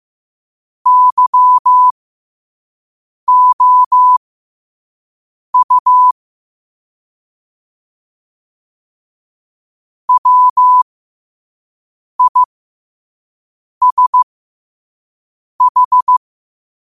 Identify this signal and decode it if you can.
Morse code, heard as a keyed tone, about 15 wpm, spaced as for 5 wpm, 1000 Hz. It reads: YOU WISH
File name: qotdfiles / 5wpm